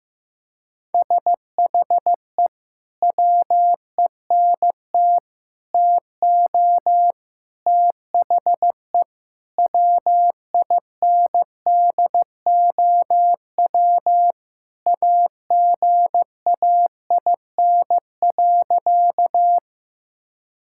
Morse code, heard as a keyed tone, about 15 wpm, 700 Hz. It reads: SHE WENT TO THE WINDOW AGAIN.
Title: Sentences from Top 1600 Words 15wpm